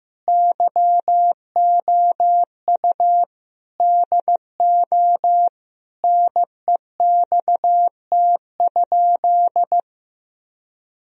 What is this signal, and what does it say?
Morse code, heard as a keyed tone, about 16 wpm, 700 Hz. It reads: YOU DO NEXT?
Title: Sentences from Top 1000 Words - Review Part 1 - 15wpm